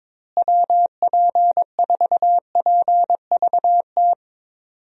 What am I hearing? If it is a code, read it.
Morse code, heard as a keyed tone, about 22 wpm, 700 Hz. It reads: WP4PVT
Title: USA Call Signs - 6 Characters - Repeated 2x - 22wpm